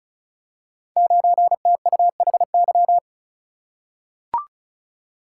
Morse code, heard as a keyed tone, about 36 wpm, 700 Hz. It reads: 9TUHY
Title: Code Groups - 5 Letter:Numbers - 35wpm